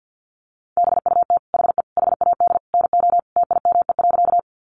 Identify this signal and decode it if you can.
Morse code, heard as a keyed tone, about 50 wpm, 700 Hz. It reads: BUT HE HAD NO TIME.
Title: Sentences from Top 100 Words 50wpm